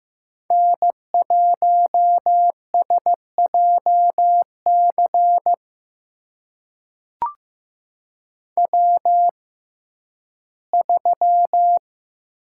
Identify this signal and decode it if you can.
Morse code, heard as a keyed tone, about 15 wpm, spaced as for 11 wpm, 700 Hz. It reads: N1SJC W3
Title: USA Call Signs - 5 Characters - 6x Character Spacing - 15wpm